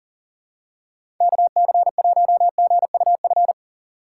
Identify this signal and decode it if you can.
Morse code, heard as a keyed tone, about 40 wpm, 700 Hz. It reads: KC1GUF
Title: USA Call Signs - 6 Characters - 2x Character Spacing - 40wpm